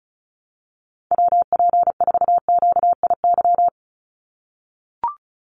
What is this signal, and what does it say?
Morse code, heard as a keyed tone, about 34 wpm, 700 Hz. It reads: WP4QIY
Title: USA Call Signs 35wpm